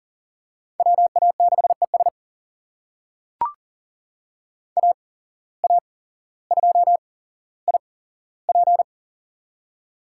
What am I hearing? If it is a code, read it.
Morse code, heard as a keyed tone, about 37 wpm, spaced as for 17 wpm, 700 Hz. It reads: WA6ES AA2IP
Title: USA Call Signs - 5 Characters - 8x Character Spacing - 40wpm